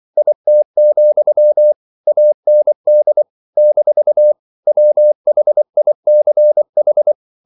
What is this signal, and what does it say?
Morse code, heard as a keyed tone, about 24 wpm, 600 Hz. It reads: IT, AND - WHICH
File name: CW / Practice / If24